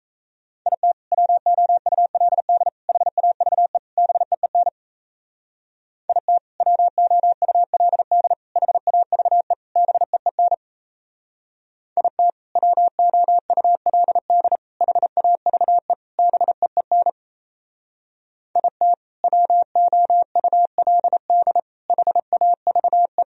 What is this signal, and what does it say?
Morse code, heard as a keyed tone, about 36 wpm, 690 Hz. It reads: IT WOULD HAVE BEEN IT WOULD HAVE BEEN IT WOULD HAVE BEEN IT WOULD HAVE